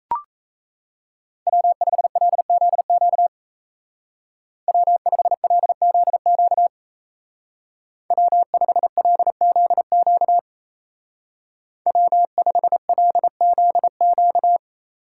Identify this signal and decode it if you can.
Morse code, heard as a keyed tone, about 33 wpm, 700 Hz. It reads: W5LZQ W5LZQ W5LZQ W5LZQ
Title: Speed Racing - USA 1x3 Call Signs 28wpm